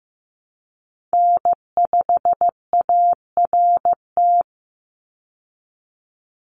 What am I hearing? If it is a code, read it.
Morse code, heard as a keyed tone, about 15 wpm, 700 Hz. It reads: N5ART
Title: USA Call Signs 15wpm